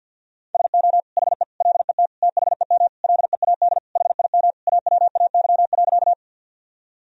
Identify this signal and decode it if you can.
Morse code, heard as a keyed tone, about 50 wpm, 700 Hz. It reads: SO HE LET THEM LEAD HIM AWAY.
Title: Sentences from Top 1000 Words - Review Part 2 - 50wpm